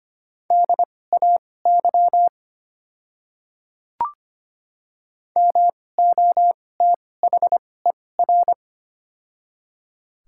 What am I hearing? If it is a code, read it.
Morse code, heard as a keyed tone, about 25 wpm, spaced as for 18 wpm, 690 Hz. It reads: DAY MOTHER
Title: 2x Character Spacing - Top 200 Words - 25wpm